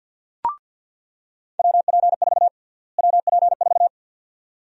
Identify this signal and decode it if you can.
Morse code, heard as a keyed tone, about 47 wpm, 700 Hz. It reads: WP4 WP4
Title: USA Call Sign Prefixes - Repeated 2x - 50wpm